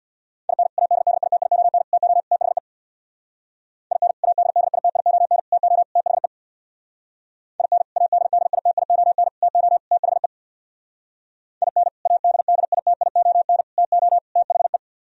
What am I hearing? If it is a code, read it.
Morse code, heard as a keyed tone, about 64 wpm, 700 Hz. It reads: IN ADDITION TO THE IN ADDITION TO THE IN ADDITION TO THE IN ADDITION TO THE